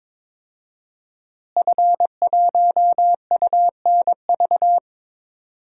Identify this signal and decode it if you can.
Morse code, heard as a keyed tone, about 21 wpm, 700 Hz. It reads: F1UNV
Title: DXCC-100 Call Signs - 5 Characters - Repeated 3x - 22wpm